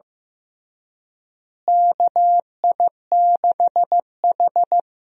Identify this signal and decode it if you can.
Morse code, heard as a keyed tone, about 15 wpm, 700 Hz. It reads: KI6H
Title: USA Call Signs - 4 Characters - Repeated 3x - 15wpm